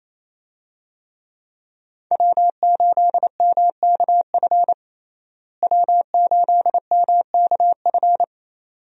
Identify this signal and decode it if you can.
Morse code, heard as a keyed tone, about 28 wpm, 700 Hz. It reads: W8MKF W8MKF
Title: USA Call Signs - 5 Characters - Repeated 2x - 28wpm